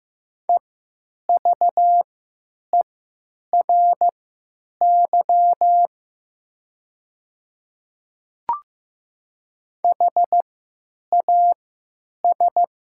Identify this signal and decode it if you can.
Morse code, heard as a keyed tone, about 15 wpm, spaced as for 9 wpm, 700 Hz. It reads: EVERY HAS
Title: 3x Character Spacing - Top 200 Words - 15wpm